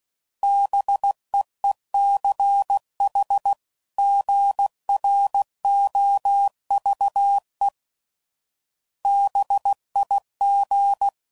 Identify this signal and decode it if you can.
Morse code, heard as a keyed tone, about 16 wpm, 800 Hz. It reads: BEECH GROVE BIG
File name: CW Code Groups @15WPM / fromSM's / Cities